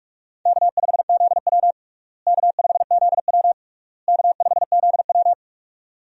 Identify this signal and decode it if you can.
Morse code, heard as a keyed tone, about 46 wpm, 700 Hz. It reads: K5ZW K5ZW K5ZW
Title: USA Call Signs - 4 Characters - Repeated 3x - 45wpm